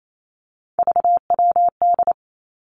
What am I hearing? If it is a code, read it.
Morse code, heard as a keyed tone, about 28 wpm, 700 Hz. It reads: VWD
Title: Three Challenging Letter-Number Combinations 28wpm